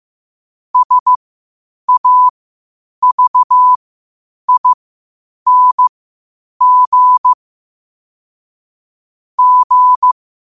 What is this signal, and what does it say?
Morse code, heard as a keyed tone, about 15 wpm, spaced as for 8 wpm, 1000 Hz. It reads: SAVING G
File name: qotdfiles / 8wpm